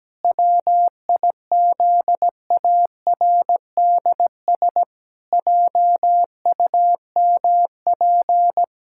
Morse code, heard as a keyed tone, about 17 wpm, 700 Hz. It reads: WIZARDS JUMP
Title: Pangram Challenge v2 - 17wpm